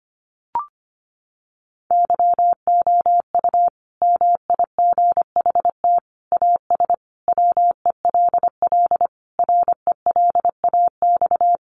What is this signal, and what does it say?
Morse code, heard as a keyed tone, about 25 wpm, 700 Hz. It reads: YOU MIGHT AS WELL RELAX
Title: Sentences from Top 1400 Words 25wpm